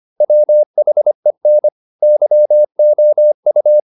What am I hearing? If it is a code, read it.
Morse code, heard as a keyed tone, about 25 wpm, 600 Hz. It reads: WHEN YOU